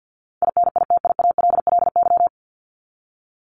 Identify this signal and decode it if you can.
Morse code, heard as a keyed tone, about 50 wpm, 700 Hz. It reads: INITIALLY
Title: Top 1251-1500 Words 50wpm